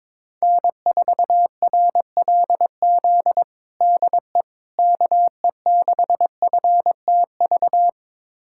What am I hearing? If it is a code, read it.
Morse code, heard as a keyed tone, about 22 wpm, 700 Hz. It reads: N4RLZ DE KE6FTV
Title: USA Callsign DE Callsign 22wpm